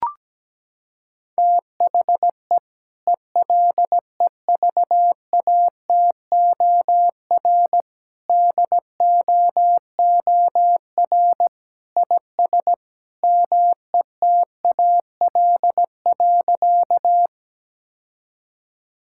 Morse code, heard as a keyed tone, about 17 wpm, 700 Hz. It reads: THE ELEVATOR DOOR IS METAL.
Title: Sentences from Top 1600 Words 17wpm